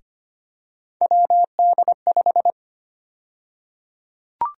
Morse code, heard as a keyed tone, about 25 wpm, 700 Hz. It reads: WD5
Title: USA Call Sign Prefixes - Repeated 2x - 25wpm